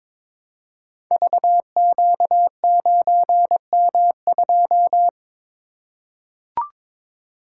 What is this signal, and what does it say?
Morse code, heard as a keyed tone, about 22 wpm, 700 Hz. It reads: VQ9M2
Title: Code Groups - 5 Letter:Numbers - 22wpm